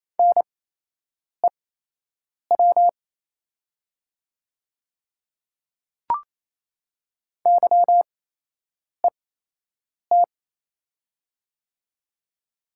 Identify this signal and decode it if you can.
Morse code, heard as a keyed tone, about 28 wpm, spaced as for 8 wpm, 700 Hz. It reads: NEW YET
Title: 8x Character Spacing - Top 500 Words - 28wpm